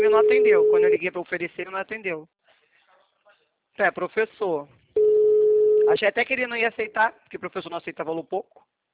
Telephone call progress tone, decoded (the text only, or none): ringback tone